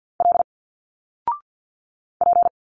Morse code, heard as a keyed tone, about 48 wpm, 700 Hz. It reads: L P
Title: Single Letters 50wpm